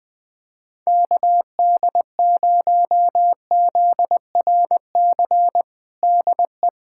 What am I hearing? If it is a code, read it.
Morse code, heard as a keyed tone, about 20 wpm, 700 Hz. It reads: KD0ZRC DE
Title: Practice QSO - Exchange 4 - Ending - 20wpm